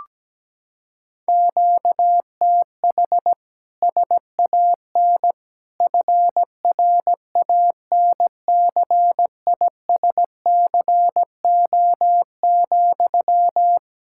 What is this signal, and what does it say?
Morse code, heard as a keyed tone, about 17 wpm, 700 Hz. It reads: QTH SAN FRANCISCO,
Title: Practice QSO - Exchange 1 - Beginning - 17wpm